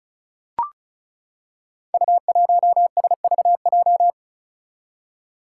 Morse code, heard as a keyed tone, about 36 wpm, 700 Hz. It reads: U1SVJ
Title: Five Challenging Letter-Number Combinations 35wpm